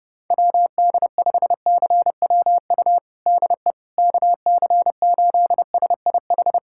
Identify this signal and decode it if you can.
Morse code, heard as a keyed tone, about 30 wpm, 700 Hz. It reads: WD5CWU DE KC8SIH
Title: USA Callsign DE Callsign 30wpm